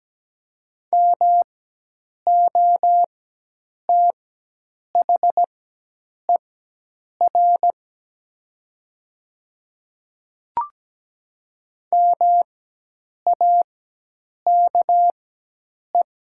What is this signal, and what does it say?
Morse code, heard as a keyed tone, about 17 wpm, spaced as for 8 wpm, 700 Hz. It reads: MOTHER MAKE
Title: 4x Character Spacing - Top 200 Words - 17wpm